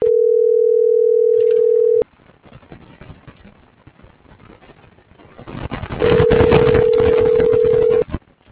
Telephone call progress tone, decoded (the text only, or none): ringback tone